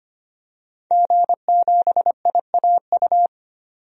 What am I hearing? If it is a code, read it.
Morse code, heard as a keyed tone, about 25 wpm, 700 Hz. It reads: G7IAU
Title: DXCC-100 Call Signs - 5 Characters - 25wpm